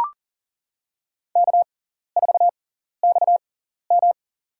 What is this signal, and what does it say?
Morse code, heard as a keyed tone, about 40 wpm, spaced as for 14 wpm, 690 Hz. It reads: K4XM
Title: USA Call Signs - 4 Characters - 6x Character Spacing - 40wpm